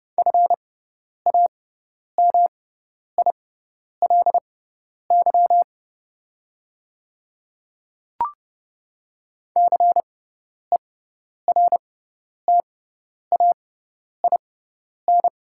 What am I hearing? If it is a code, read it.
Morse code, heard as a keyed tone, about 30 wpm, spaced as for 10 wpm, 690 Hz. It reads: FAMILY CERTAIN
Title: 6x Character Spacing - Top 500 Words - 30wpm